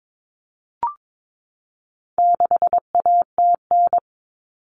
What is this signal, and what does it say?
Morse code, heard as a keyed tone, about 22 wpm, 700 Hz. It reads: 6ATN
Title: All Cut Numbers - 4-Digit 22wpm